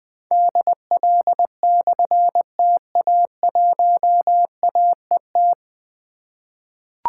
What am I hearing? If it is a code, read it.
Morse code, heard as a keyed tone, about 20 wpm, 700 Hz. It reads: DL/TA1AET
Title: Difficult Call Signs 20wpm